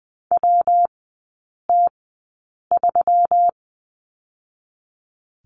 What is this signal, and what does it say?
Morse code, heard as a keyed tone, about 20 wpm, spaced as for 8 wpm, 700 Hz. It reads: WT3
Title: 121 Lesson 20e - Number 3 - Three Characters 20wpm